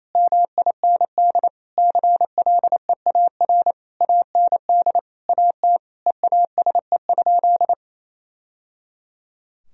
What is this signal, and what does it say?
Morse code, heard as a keyed tone, about 28 wpm, 700 Hz. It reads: MIND CLEAR AND AT EASE?
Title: Sentences from Top 1000 Words - Review Part 1 - 28wpm